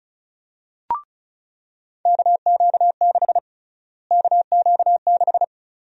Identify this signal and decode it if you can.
Morse code, heard as a keyed tone, about 34 wpm, 700 Hz. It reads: KQ6 KQ6
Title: USA Call Sign Prefixes - Repeated 2x - 35wpm